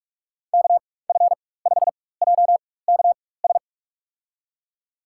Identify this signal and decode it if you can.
Morse code, heard as a keyed tone, about 46 wpm, spaced as for 21 wpm, 690 Hz. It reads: KF5JKS
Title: USA Call Signs - 4x Character Spacing - 45wpm